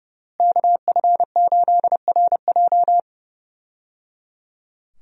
Morse code, heard as a keyed tone, about 30 wpm, 700 Hz. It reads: KF8RJ
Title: USA Call Signs - 5 Characters - Repeated 3x - 30wpm